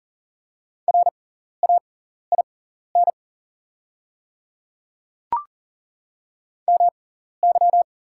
Morse code, heard as a keyed tone, about 37 wpm, spaced as for 13 wpm, 700 Hz. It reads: RAIN MY
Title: 6x Character Spacing - Top 300 Words - 40wpm